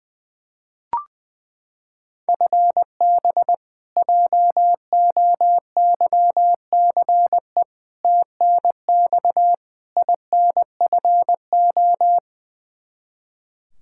Morse code, heard as a keyed tone, about 20 wpm, 700 Hz. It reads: FB JOYCE TNX INFO
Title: K1USN SST QSO Practice 20wpm